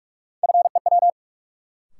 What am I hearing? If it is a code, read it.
Morse code, heard as a keyed tone, about 42 wpm, 700 Hz. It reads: FEW
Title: Sentences from Top 300 Words - New Word Drill - 45wpm